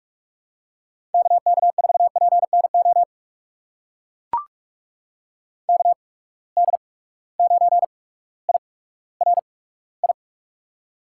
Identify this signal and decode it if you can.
Morse code, heard as a keyed tone, about 44 wpm, spaced as for 19 wpm, 700 Hz. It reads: KK4PNO KD9IRI
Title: USA Call Signs - 6 Characters - 8x Character Spacing - 45wpm